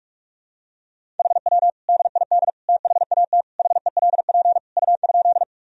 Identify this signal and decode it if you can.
Morse code, heard as a keyed tone, about 46 wpm, 700 Hz. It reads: HW DID THAT HELP U?
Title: Sentences from Top 200 Words and Abbreviations 45wpm